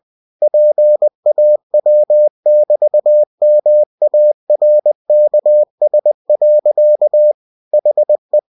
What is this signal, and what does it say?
Morse code, heard as a keyed tone, about 20 wpm, 600 Hz. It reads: PAW<BT>MARKS. HE
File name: winnie-20wpm